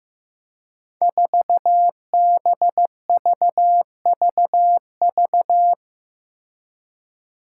Morse code, heard as a keyed tone, about 15 wpm, 700 Hz. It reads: 4BVVV
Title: Five Challenging Letter-Number Combinations 15wpm